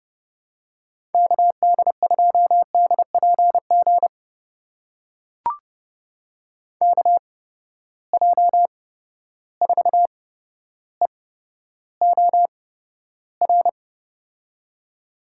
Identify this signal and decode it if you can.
Morse code, heard as a keyed tone, about 30 wpm, spaced as for 13 wpm, 700 Hz. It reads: KD2DPG KJ4EOR
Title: USA Call Signs - 6 Characters - 8x Character Spacing - 30wpm